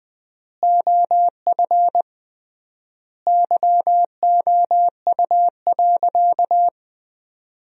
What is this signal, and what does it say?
Morse code, heard as a keyed tone, about 20 wpm, 700 Hz. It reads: OF YOU.
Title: 3x Word-Spacing - Sentences from Top 400 Words - 20wpm